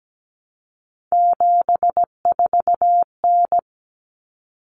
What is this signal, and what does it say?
Morse code, heard as a keyed tone, about 17 wpm, 700 Hz. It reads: 74N
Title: Common Cut Numbers - 3-Digit 17wpm